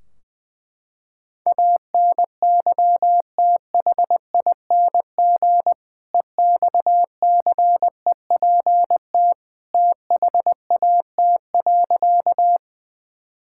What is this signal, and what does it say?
Morse code, heard as a keyed tone, about 20 wpm, 700 Hz. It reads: ANYTHING EXCEPT THAT.
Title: Sentences from Top 1600 Words 20wpm